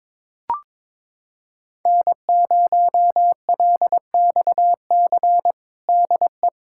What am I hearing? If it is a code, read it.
Morse code, heard as a keyed tone, about 22 wpm, 700 Hz. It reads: N0LXC DE
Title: USA Callsign DE Callsign - 5 Characters - 22wpm